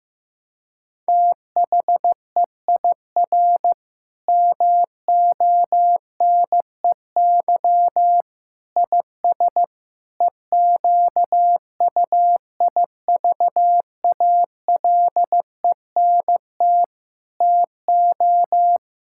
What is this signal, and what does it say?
Morse code, heard as a keyed tone, about 15 wpm, 700 Hz. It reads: THEIR MONEY IS EQUIVALENT TO
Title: Sentences from Top 1600 Words 15wpm